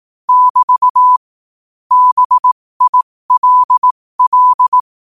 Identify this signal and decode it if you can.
Morse code, heard as a keyed tone, about 18 wpm, spaced as for 15 wpm, 1000 Hz. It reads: <BT> BILL
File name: qotdfiles / 15wpm